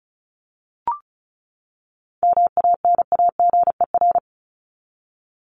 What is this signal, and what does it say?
Morse code, heard as a keyed tone, about 34 wpm, 700 Hz. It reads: MANAGER
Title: Top 1501-1750 Words 35wpm